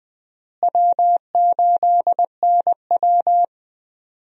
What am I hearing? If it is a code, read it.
Morse code, heard as a keyed tone, about 20 wpm, 700 Hz. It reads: W8NW
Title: USA Call Signs - 4 Characters - Repeated 3x - 20wpm